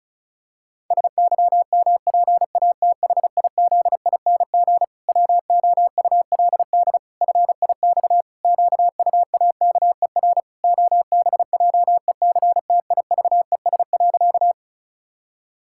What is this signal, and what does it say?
Morse code, heard as a keyed tone, about 36 wpm, 700 Hz. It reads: SYMPATHIZING WOULD FIX QUAKER OBJECTIVES.
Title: Pangram Challenge v2 - 35wpm